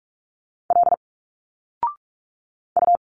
Morse code, heard as a keyed tone, about 46 wpm, 700 Hz. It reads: L U
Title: Single Letters 45wpm